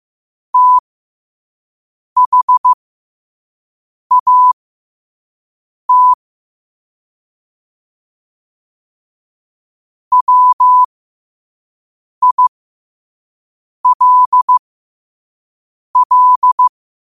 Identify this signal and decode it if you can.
Morse code, heard as a keyed tone, about 15 wpm, spaced as for 5 wpm, 1000 Hz. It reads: THAT WILL
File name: qotdfiles / 5wpm